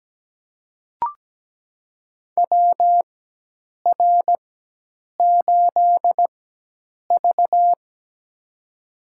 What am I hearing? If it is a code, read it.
Morse code, heard as a keyed tone, about 17 wpm, spaced as for 8 wpm, 700 Hz. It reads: WR8V
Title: USA Call Signs - 4 Characters - 4x Character Spacing - 17wpm